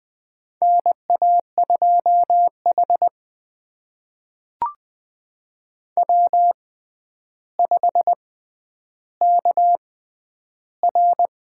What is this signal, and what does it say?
Morse code, heard as a keyed tone, about 20 wpm, spaced as for 10 wpm, 700 Hz. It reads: NA2H W5KR
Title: USA Call Signs - 6x Character Spacing - 20wpm